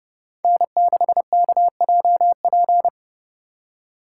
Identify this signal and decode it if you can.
Morse code, heard as a keyed tone, about 34 wpm, 700 Hz. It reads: N6KJP
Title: Speed Racing - USA 1x3 Call Signs 30wpm